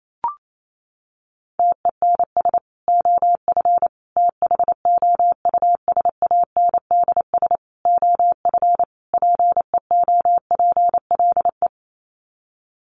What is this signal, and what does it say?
Morse code, heard as a keyed tone, about 28 wpm, 700 Hz. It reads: TENS OF THOUSANDS OF PEOPLE
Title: Sets of 5 Words Encore2 28wpm